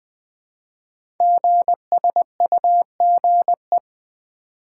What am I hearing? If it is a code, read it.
Morse code, heard as a keyed tone, about 20 wpm, 700 Hz. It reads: GSUGE
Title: Code Groups - 5 Letters - 20wpm